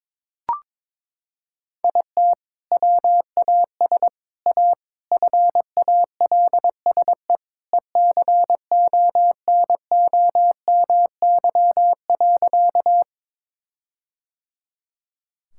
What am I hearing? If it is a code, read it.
Morse code, heard as a keyed tone, about 22 wpm, 700 Hz. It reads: IT WAS A FALSE ECONOMY.
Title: Sentences from Top 1600 Words 22wpm